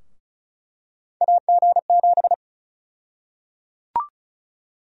Morse code, heard as a keyed tone, about 34 wpm, 700 Hz. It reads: AG7
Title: USA Call Sign Prefixes - 35wpm